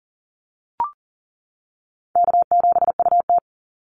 Morse code, heard as a keyed tone, about 40 wpm, 700 Hz. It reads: K7UT
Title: Easy Call Signs - 40wpm